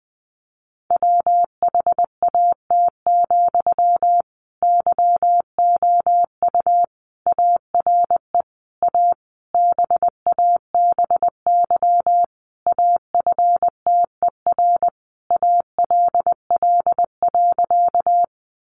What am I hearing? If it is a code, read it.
Morse code, heard as a keyed tone, about 20 wpm, 700 Hz. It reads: WHAT, YOU ARE A BABY AFTER ALL.
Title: Sentences from Top 1300 Words 20wpm